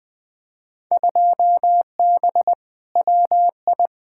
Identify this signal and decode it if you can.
Morse code, heard as a keyed tone, about 20 wpm, 700 Hz. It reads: 2B WI
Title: Field Day Exchanges - 20wpm